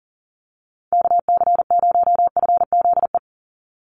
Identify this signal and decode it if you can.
Morse code, heard as a keyed tone, about 40 wpm, 700 Hz. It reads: KC0FZE
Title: USA Call Signs 40wpm